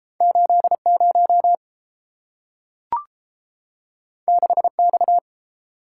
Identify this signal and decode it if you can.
Morse code, heard as a keyed tone, about 30 wpm, 700 Hz. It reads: 80 6X